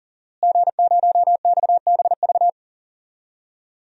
Morse code, heard as a keyed tone, about 37 wpm, 700 Hz. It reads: G0XBV
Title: DXCC-100 Call Signs - 5 Characters - Repeated 3x - 40wpm